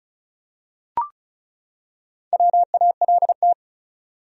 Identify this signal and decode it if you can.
Morse code, heard as a keyed tone, about 34 wpm, 700 Hz. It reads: WALT
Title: USA Common Names - 35wpm